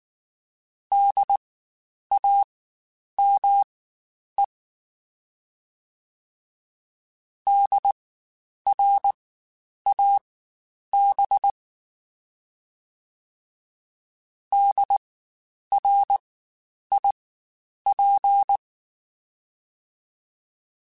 Morse code, heard as a keyed tone, about 19 wpm, spaced as for 9 wpm, 800 Hz. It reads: DAME DRAB DRIP
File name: Spaced Alphas / Spaced Alphas mp3s / Ds sp @18